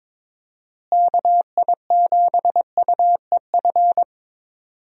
Morse code, heard as a keyed tone, about 22 wpm, 700 Hz. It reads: KI7UEF
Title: USA Call Signs - 6 Characters - Repeated 3x - 22wpm